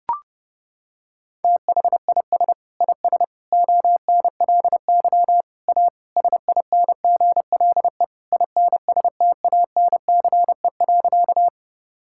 Morse code, heard as a keyed tone, about 32 wpm, 700 Hz. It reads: THIS IS ONLY A SINGLE INSTANCE.
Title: Sentences from Top 1000 Words - Review Part 5 - 30wpm